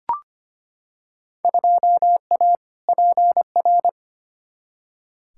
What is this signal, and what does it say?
Morse code, heard as a keyed tone, about 25 wpm, 700 Hz. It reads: 2A PR
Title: Field Day Exchanges Encore - 25wpm